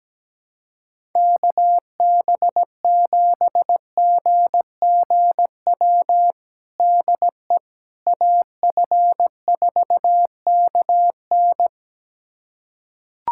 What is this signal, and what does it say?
Morse code, heard as a keyed tone, about 17 wpm, 700 Hz. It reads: KB7GGW DE AF4KN
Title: USA Callsign DE Callsign 17wpm